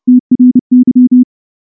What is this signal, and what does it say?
Morse code, heard as a keyed tone, about 28 wpm, 250 Hz. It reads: TRY
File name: morse / morse-training-30WPM